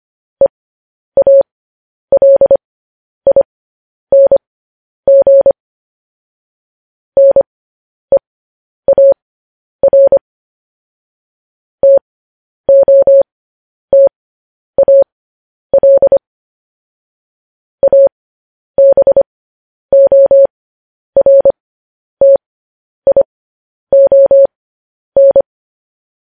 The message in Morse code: EALING NEAR TOTAL ABORTION

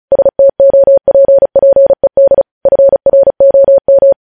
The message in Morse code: STOPPED FROM

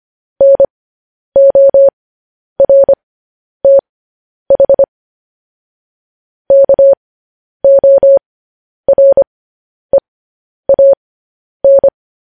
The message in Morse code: NORTH KOREAN

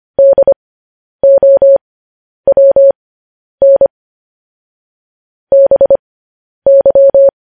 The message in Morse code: DOWN BY

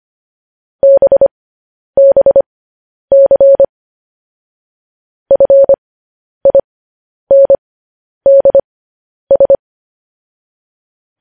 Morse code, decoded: BBC FINDS

